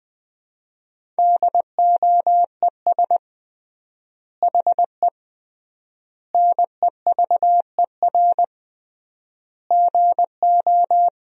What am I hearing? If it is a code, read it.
Morse code, heard as a keyed tone, about 20 wpm, 700 Hz. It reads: DOES HE NEVER GO
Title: 3x Word-Spacing - Sentences from Top 300 Words - 20wpm